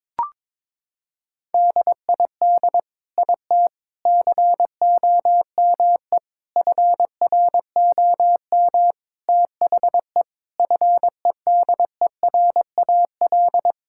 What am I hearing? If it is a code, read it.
Morse code, heard as a keyed tone, about 21 wpm, 700 Hz. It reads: DID IT COME FROM THE FEDERAL
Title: Sentences from Top 1000 Words - Review Part 5 - 22wpm